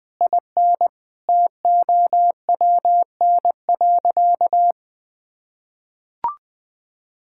Morse code, heard as a keyed tone, about 20 wpm, 700 Hz. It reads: IN TOWN.
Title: Sentences from Top 1600 Words 20wpm